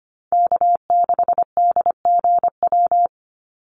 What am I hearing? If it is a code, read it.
Morse code, heard as a keyed tone, about 25 wpm, 700 Hz. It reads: K6DGW
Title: Call Signs Repeated 3x - 25wpm